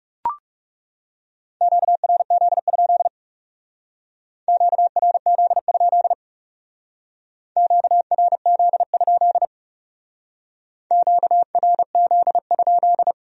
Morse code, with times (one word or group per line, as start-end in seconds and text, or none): 1.6-3.1 QRZ?
4.5-6.1 QRZ?
7.6-9.4 QRZ?
10.9-13.1 QRZ?